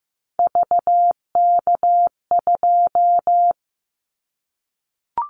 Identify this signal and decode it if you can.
Morse code, heard as a keyed tone, about 15 wpm, 700 Hz. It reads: VK2
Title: Callsign Prefixes for Top 100 DXCC - 15wpm